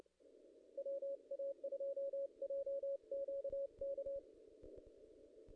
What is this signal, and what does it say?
Morse code, heard as a keyed tone, about 30 wpm, 550 Hz. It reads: WA2JQK
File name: contests / ss-cw-k5tr-op-n5rz -0000